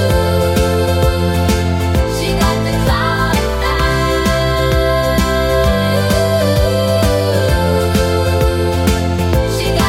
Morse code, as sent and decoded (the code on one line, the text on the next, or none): none